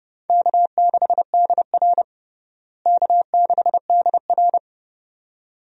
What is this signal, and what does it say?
Morse code, heard as a keyed tone, about 30 wpm, 700 Hz. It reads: K6DR K6DR
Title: USA Call Signs - 4 Characters - Repeated 2x - 30wpm